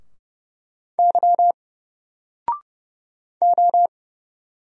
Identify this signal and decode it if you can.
Morse code, heard as a keyed tone, about 30 wpm, 700 Hz. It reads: Y O
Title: Single Character (Letter:Number:Punctuation:Prosign) 30wpm